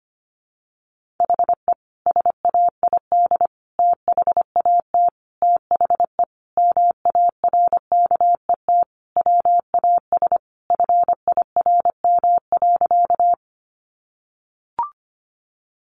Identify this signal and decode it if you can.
Morse code, heard as a keyed tone, about 25 wpm, 700 Hz. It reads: HE SAID THAT THE MARKET WAS FIRM.
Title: Sentences from Top 1400 Words 25wpm